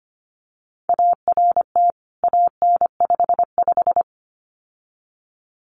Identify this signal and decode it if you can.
Morse code, heard as a keyed tone, about 25 wpm, 700 Hz. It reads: ART AN55
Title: CWT Exchange 25wpm